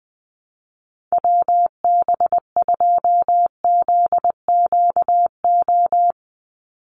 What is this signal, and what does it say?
Morse code, heard as a keyed tone, about 20 wpm, 700 Hz. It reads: WB2ZQO
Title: USA Call Signs 20wpm